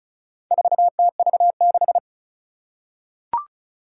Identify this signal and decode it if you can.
Morse code, heard as a keyed tone, about 36 wpm, 700 Hz. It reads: 4TV6
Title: Four Challenging Letter-Number Combinations 35wpm